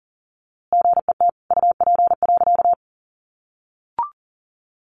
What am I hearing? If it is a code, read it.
Morse code, heard as a keyed tone, about 40 wpm, 700 Hz. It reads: GET UP.
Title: A Sound Of Thunder 40wpm